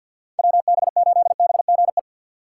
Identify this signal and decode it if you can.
Morse code, heard as a keyed tone, about 60 wpm, 700 Hz. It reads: WB8BGE
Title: Speed Racing - USA 2x3 Call Signs 50wpm